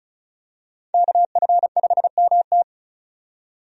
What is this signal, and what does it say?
Morse code, heard as a keyed tone, about 36 wpm, 700 Hz. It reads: KF5MT
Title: USA Call Signs - 5 Characters - 35wpm